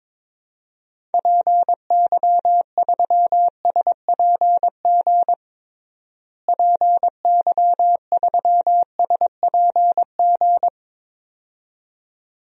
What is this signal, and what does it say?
Morse code, heard as a keyed tone, about 22 wpm, 700 Hz. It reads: PY3SPG PY3SPG
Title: DXCC-100 Call Signs - 6 Characters - Repeated 2x - 22wpm